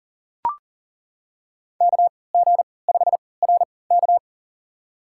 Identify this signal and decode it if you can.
Morse code, heard as a keyed tone, about 39 wpm, spaced as for 23 wpm, 700 Hz. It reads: KG5RK
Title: USA Call Signs - 5 Characters - 3x Character Spacing - 40wpm